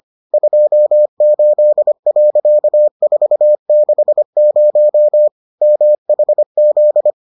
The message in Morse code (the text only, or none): 28.460 MHZ